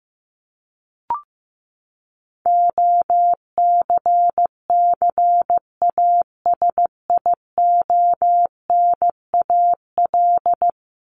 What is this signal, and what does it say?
Morse code, heard as a keyed tone, about 15 wpm, 700 Hz. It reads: OCCASIONAL
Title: LY Suffix - Top 100 Words with Base Words 15wpm